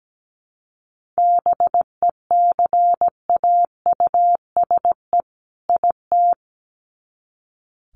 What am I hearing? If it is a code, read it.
Morse code, heard as a keyed tone, about 17 wpm, 700 Hz. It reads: BECAUSE IT
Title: Sets of 2 Words 17wpm